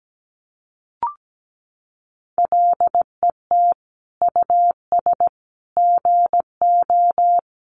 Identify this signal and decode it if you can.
Morse code, heard as a keyed tone, about 17 wpm, 700 Hz. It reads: LET US GO
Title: Sentences from Top 300 Words 17wpm